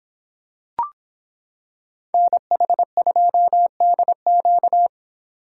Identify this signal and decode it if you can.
Morse code, heard as a keyed tone, about 26 wpm, 700 Hz. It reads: NH2DQ